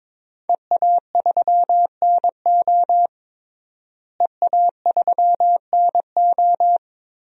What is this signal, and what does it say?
Morse code, heard as a keyed tone, about 22 wpm, 700 Hz. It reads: EA3NO EA3NO
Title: DXCC-100 Call Signs - 5 Characters - Repeated 2x - 22wpm